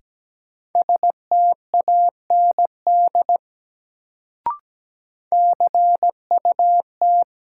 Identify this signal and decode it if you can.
Morse code, heard as a keyed tone, about 17 wpm, 700 Hz. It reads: STAND CUT
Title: Sentences from Top 300 Words - New Word Drill - 17wpm